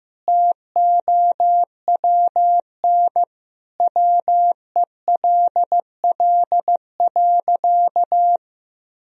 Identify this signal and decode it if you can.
Morse code, heard as a keyed tone, about 16 wpm, 700 Hz. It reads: TOWN WELL.
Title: Sentences from Top 1000 Words - Review Part 2 - 15wpm